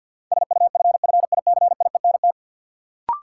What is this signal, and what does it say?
Morse code, heard as a keyed tone, about 49 wpm, 700 Hz. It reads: SUFFICIENT
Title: Sentences from Top 1400 Words - New Word Drill Alt - 50wpm